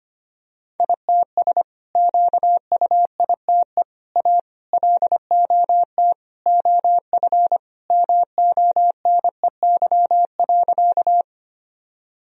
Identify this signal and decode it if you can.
Morse code, heard as a keyed tone, about 25 wpm, 700 Hz. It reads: ITS QUITE A LOT OF MONEY.
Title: Sentences from Top 1000 Words - Review Part 3 - 25wpm